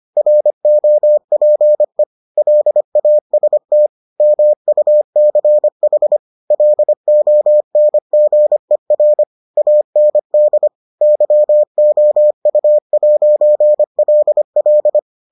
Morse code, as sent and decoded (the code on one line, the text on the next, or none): .-. --- .--. / .-.. .- ... - / -- ..- -.-. .... / .-.. --- -. --. .-. / .- -. -.. / -.-- --- ..- .----. .-.. .-..
ROPE LAST MUCH LONGER AND YOU'LL